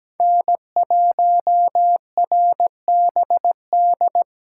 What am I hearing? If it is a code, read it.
Morse code, heard as a keyed tone, about 17 wpm, 700 Hz. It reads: N1RBD
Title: K1USN SST QSO Practice - 17wpm